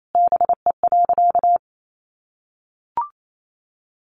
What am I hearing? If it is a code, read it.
Morse code, heard as a keyed tone, about 28 wpm, 700 Hz. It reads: BE.
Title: Sentences from Top 900 Words 28wpm